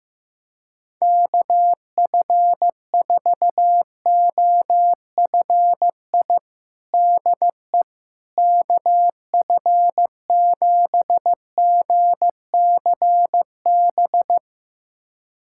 Morse code, heard as a keyed tone, about 15 wpm, 700 Hz. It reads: KF4OFI DE KF7GCB
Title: USA Callsign DE Callsign - 6 Characters - 15wpm